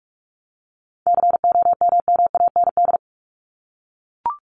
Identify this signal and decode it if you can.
Morse code, heard as a keyed tone, about 44 wpm, 700 Hz. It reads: COMMAND
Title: Top 1501-1750 Words 45wpm